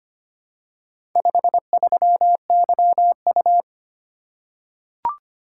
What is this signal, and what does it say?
Morse code, heard as a keyed tone, about 25 wpm, 700 Hz. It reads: 53YU
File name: practice / 25wpm / Four Letter-Number Combinations - 25wpm